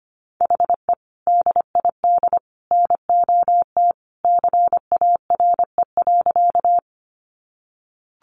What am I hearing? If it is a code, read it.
Morse code, heard as a keyed tone, about 25 wpm, 700 Hz. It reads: HE DID NOT CARE.
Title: Sentences from Top 600 Words 25wpm